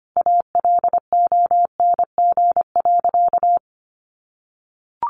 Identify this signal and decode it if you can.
Morse code, heard as a keyed tone, about 25 wpm, 700 Hz. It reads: ALONG.
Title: Sentences from Top 1500 Words 25wpm